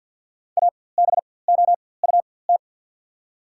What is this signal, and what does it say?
Morse code, heard as a keyed tone, about 52 wpm, spaced as for 24 wpm, 690 Hz. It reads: ABOUT
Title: 4x Character Spacing - Top 500 Words - 50wpm